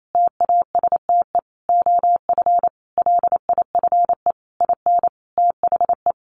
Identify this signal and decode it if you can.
Morse code, heard as a keyed tone, about 28 wpm, 700 Hz. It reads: TASTE OF LIFE IN THE
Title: Sentences from Top 1100 Words 28wpm